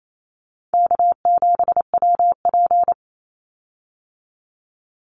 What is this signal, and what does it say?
Morse code, heard as a keyed tone, about 28 wpm, 700 Hz. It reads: K7WP
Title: Easy Call Signs Repeated 2x - 28wpm